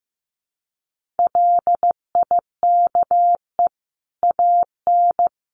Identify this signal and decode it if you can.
Morse code, heard as a keyed tone, about 15 wpm, 700 Hz. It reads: LIKE AN
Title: Sentences from Top 1500 Words 15wpm